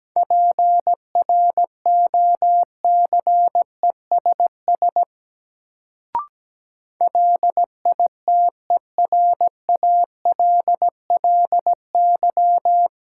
Morse code, heard as a keyed tone, about 17 wpm, 700 Hz. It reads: PROCESS LITERALLY
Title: Sentences from Top 1400 Words - New Word Drill Alt - 17wpm